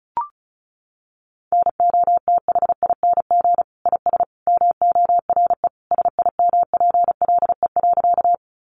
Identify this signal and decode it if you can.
Morse code, heard as a keyed tone, about 34 wpm, 700 Hz. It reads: NOTHING IS MORE SIMPLE.
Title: Sentences from Top 500 Words 35wpm